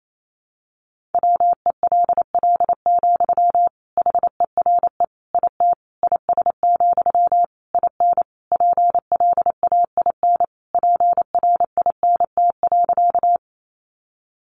Morse code, heard as a keyed tone, about 28 wpm, 700 Hz. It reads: WELL, HERE IT IS, IN PLAIN PRINT.
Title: Sentences from Top 1400 Words 28wpm